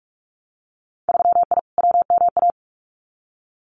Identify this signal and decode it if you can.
Morse code, heard as a keyed tone, about 46 wpm, 690 Hz. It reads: 3I WMA
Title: Winter Field Day Exchanges Encore 45wpm